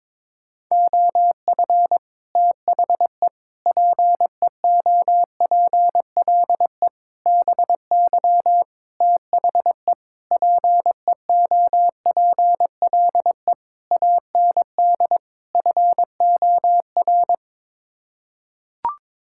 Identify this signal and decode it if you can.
Morse code, heard as a keyed tone, about 22 wpm, 700 Hz. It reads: OF THE PEOPLE BY THE PEOPLE AND FOR
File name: practice / 22wpm / Sets of 8 Words - 22wpm